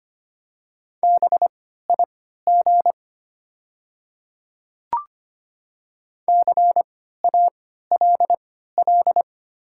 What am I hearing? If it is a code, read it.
Morse code, heard as a keyed tone, about 25 wpm, spaced as for 14 wpm, 690 Hz. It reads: BIG CALL
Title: 3x Character Spacing - Top 300 Words - 25wpm